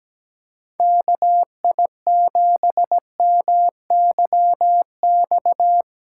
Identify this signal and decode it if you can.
Morse code, heard as a keyed tone, about 17 wpm, 700 Hz. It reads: KI7MYX
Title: USA Call Signs - 6 Characters - 6x Character Spacing - 17wpm